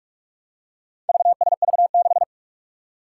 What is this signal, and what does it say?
Morse code, heard as a keyed tone, about 46 wpm, 700 Hz. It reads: VSV6
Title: Four Challenging Letter-Number Combinations 45wpm